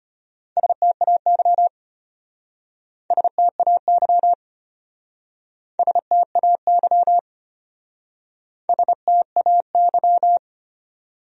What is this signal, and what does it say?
Morse code, heard as a keyed tone, about 33 wpm, 700 Hz. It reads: STAY STAY STAY STAY